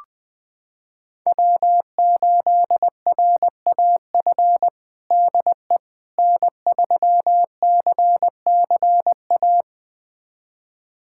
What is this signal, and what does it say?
Morse code, heard as a keyed tone, about 20 wpm, 700 Hz. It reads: W8RAF DE N3CCA
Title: USA Callsign DE Callsign - 5 Characters - 20wpm